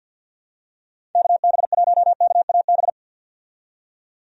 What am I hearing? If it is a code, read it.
Morse code, heard as a keyed tone, about 47 wpm, 700 Hz. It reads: KB1KAB
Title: USA Call Signs - 6 Characters - 50wpm